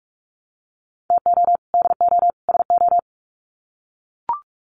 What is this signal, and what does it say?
Morse code, heard as a keyed tone, about 44 wpm, 700 Hz. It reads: TO DO SO
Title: Sets of 3 Words 45wpm